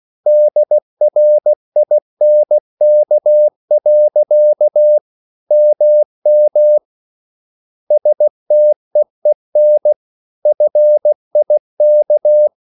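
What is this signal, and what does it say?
Morse code, heard as a keyed tone, about 16 wpm, 600 Hz. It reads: DRINK. MM STEEN FIK
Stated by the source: Slow (QRS) Morse code at 16 wpm, from CW med Gnister.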